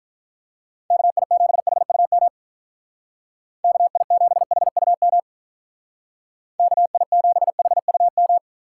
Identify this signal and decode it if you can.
Morse code, heard as a keyed tone, about 47 wpm, 700 Hz. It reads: KI7HUM KI7HUM KI7HUM